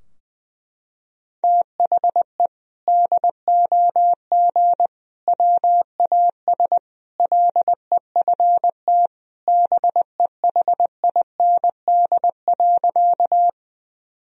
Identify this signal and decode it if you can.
Morse code, heard as a keyed tone, about 21 wpm, 700 Hz. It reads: THE DOG WAS LEFT BEHIND.
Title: Sentences from Top 1000 Words - Review Part 2 - 20wpm